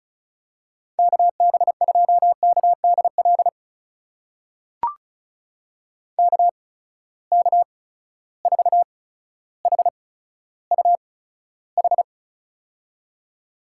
Morse code, heard as a keyed tone, about 36 wpm, spaced as for 15 wpm, 700 Hz. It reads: KB2KDL KK4HUH
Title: USA Call Signs - 6 Characters - 8x Character Spacing - 35wpm